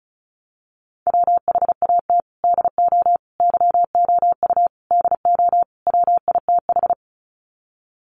Morse code, heard as a keyed tone, about 34 wpm, 700 Hz. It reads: WHAT DO YOU DO WITH
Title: Sets of 5 Words Encore2 35wpm